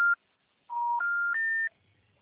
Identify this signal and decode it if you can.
Special information tone.
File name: out- -2017